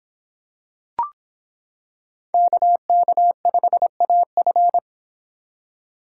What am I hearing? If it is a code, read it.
Morse code, heard as a keyed tone, about 26 wpm, 700 Hz. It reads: KK5AF